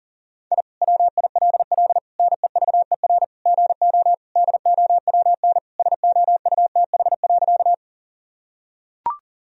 Morse code, heard as a keyed tone, about 40 wpm, 700 Hz. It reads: I WILL NEVER GO DOWN SOUTH.
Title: Sentences from Top 1000 Words - Review Part 2 - 40wpm